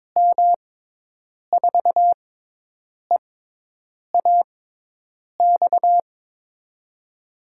Morse code, heard as a keyed tone, about 22 wpm, spaced as for 8 wpm, 700 Hz. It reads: M4EAX
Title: USA Call Signs - 6 Characters - 6x Character Spacing - 22wpm